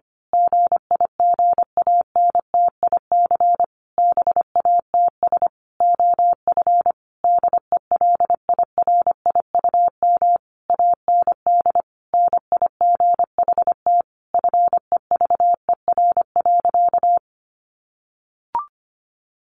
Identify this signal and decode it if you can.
Morse code, heard as a keyed tone, about 25 wpm, 700 Hz. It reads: GIGANTIC BATS OF DELIRIUM AND NIGHT FEVER.
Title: A Sound Of Thunder 25wpm